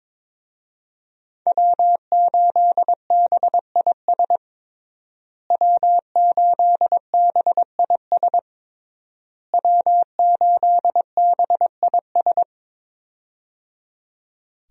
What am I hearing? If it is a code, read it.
Morse code, heard as a keyed tone, about 22 wpm, 700 Hz. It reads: W8BIS W8BIS W8BIS
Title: USA Call Signs - 5 Characters - Repeated 3x - 22wpm